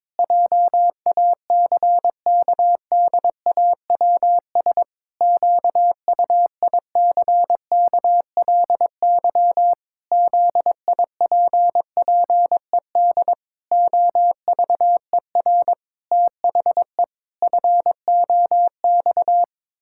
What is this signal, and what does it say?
Morse code, heard as a keyed tone, about 22 wpm, 700 Hz. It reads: JACKDAWS QUICKLY ZIPPED OVER THE FOX
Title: Pangram Challenge v2 - 22wpm